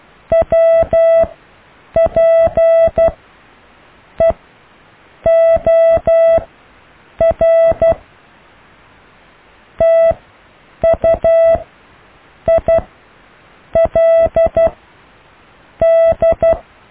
Morse code, heard as a keyed tone, about 12 wpm, spaced as for 7 wpm, 660 Hz. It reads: WPEOR TUILD
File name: cw / IJLUM+CDRO+EKTPV+BFHNW